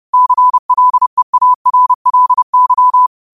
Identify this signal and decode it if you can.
Morse code, heard as a keyed tone, about 30 wpm, 1000 Hz. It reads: CLEARLY
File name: qotdfiles / 30wpm